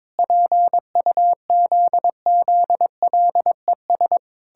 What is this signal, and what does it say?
Morse code, heard as a keyed tone, about 22 wpm, 700 Hz. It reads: PUZZLES
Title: Pangram Challenge v2 - 22wpm